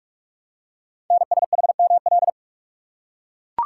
Morse code, heard as a keyed tone, about 44 wpm, 700 Hz. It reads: NSHML
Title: Code Groups - 5 Letters - 45wpm